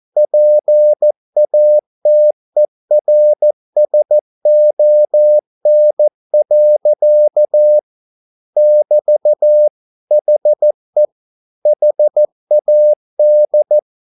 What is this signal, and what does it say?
Morse code, heard as a keyed tone, about 14 wpm, 600 Hz. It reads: PATERSON. <BT> HE HAD
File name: mp3 files / a dogs mistake - 14wpm